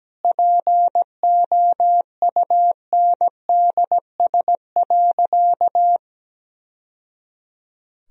Morse code, heard as a keyed tone, about 17 wpm, 700 Hz. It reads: POUNDS.
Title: Sentences from Top 1000 Words - Review Part 2 - 17wpm